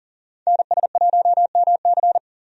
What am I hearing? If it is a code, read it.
Morse code, heard as a keyed tone, about 37 wpm, 700 Hz. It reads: NS1MC
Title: USA Call Signs - 5 Characters - 40wpm